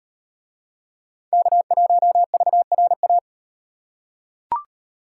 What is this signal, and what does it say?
Morse code, heard as a keyed tone, about 35 wpm, 700 Hz. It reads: K1VRA